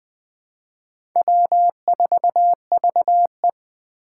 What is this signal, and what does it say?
Morse code, heard as a keyed tone, about 20 wpm, 700 Hz. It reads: W4VE
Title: Four Challenging Letter-Number Combinations 20wpm